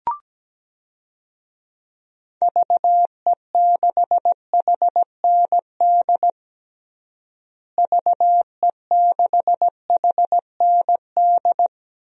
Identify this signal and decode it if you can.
Morse code, heard as a keyed tone, about 17 wpm, 700 Hz. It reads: VE6HND VE6HND
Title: DXCC-100 Call Signs - 6 Characters - Repeated 2x - 17wpm